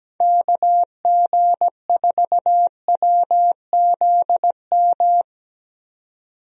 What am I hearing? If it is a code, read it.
Morse code, heard as a keyed tone, about 17 wpm, 700 Hz. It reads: KG4WZM
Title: USA Call Signs - 6 Characters - 17wpm